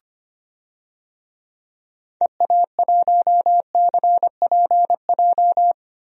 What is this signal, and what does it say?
Morse code, heard as a keyed tone, about 25 wpm, 700 Hz. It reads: EA1CPJ
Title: DXCC-100 Call Signs - 6 Characters - 25wpm